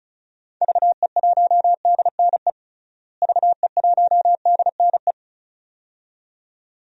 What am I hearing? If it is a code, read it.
Morse code, heard as a keyed tone, about 34 wpm, 700 Hz. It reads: VE1DNE VE1DNE
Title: DXCC-100 Call Signs - 6 Characters - Repeated 2x - 35wpm